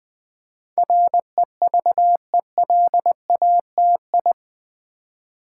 Morse code, heard as a keyed tone, about 20 wpm, 700 Hz. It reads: REVELATI
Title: Build A Word - 4 Letters At A Time - 10-Letter Words - 20wpm